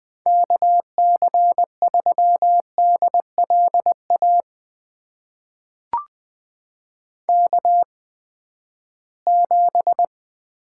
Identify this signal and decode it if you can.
Morse code, heard as a keyed tone, about 20 wpm, spaced as for 14 wpm, 700 Hz. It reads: KC3DLA K7
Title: USA Call Signs - 8x Character Spacing - 20wpm